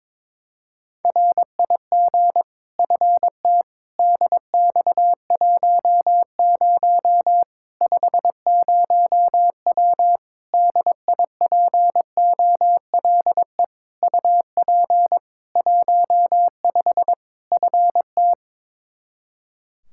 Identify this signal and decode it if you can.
Morse code, heard as a keyed tone, about 21 wpm, 700 Hz. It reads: RIG FT DX10 50W DIPOLE UP 15 FT
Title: K1USN SST QSO Practice 22wpm